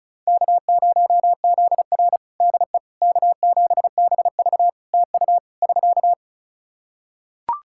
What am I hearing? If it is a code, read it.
Morse code, heard as a keyed tone, about 34 wpm, 700 Hz. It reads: K0ZR DE K7BV TU <SK>
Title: Practice QSO - Exchange 4 - Ending - 35wpm